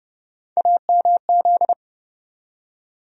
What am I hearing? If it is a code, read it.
Morse code, heard as a keyed tone, about 30 wpm, 700 Hz. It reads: AMZ
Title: USA Call Sign Postfixes - 3 Letters 30wpm